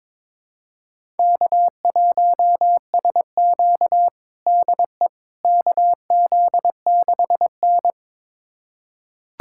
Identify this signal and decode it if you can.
Morse code, heard as a keyed tone, about 23 wpm, 700 Hz. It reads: K1SQ DE KZ6N
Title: USA Callsign DE Callsign - 4 Characters - 22wpm